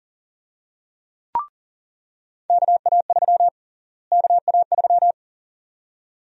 Morse code, heard as a keyed tone, about 40 wpm, 700 Hz. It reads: KA3 KA3
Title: USA Call Sign Prefixes - Repeated 2x - 40wpm